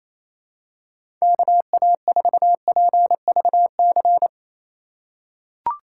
Morse code, heard as a keyed tone, about 28 wpm, 700 Hz. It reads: KA4PVC
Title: USA Call Signs - 6 Characters - 2x Character Spacing - 28wpm